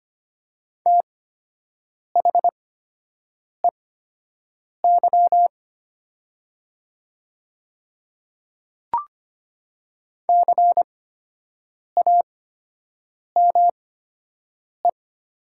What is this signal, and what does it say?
Morse code, heard as a keyed tone, about 25 wpm, spaced as for 7 wpm, 700 Hz. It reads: THEY CAME
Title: 8x Character Spacing - Top 200 Words - 25wpm